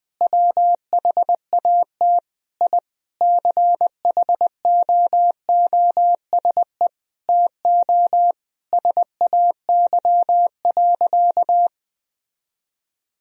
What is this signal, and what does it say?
Morse code, heard as a keyed tone, about 21 wpm, 700 Hz. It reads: WHAT I CHOOSE TO SAY.
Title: Sentences from Top 1000 Words - Review Part 4 - 20wpm